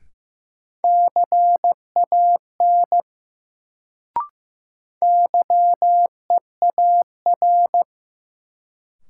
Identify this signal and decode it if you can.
Morse code, heard as a keyed tone, about 15 wpm, 700 Hz. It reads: CAN YEAR
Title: Sets of 2 Words - New Word Drill - 15wpm